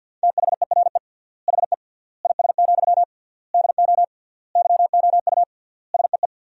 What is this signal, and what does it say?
Morse code, heard as a keyed tone, about 50 wpm, 700 Hz. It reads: THERE HE IS, DO YOU SEE
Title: 3x Word-Spacing - Sentences from Top 100 Words - 50wpm